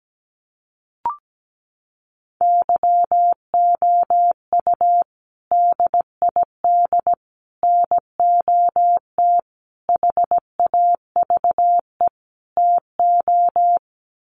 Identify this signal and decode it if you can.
Morse code, heard as a keyed tone, about 17 wpm, 700 Hz. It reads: YOU DID NOT HAVE TO
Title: Sets of 5 Words Encore 17wpm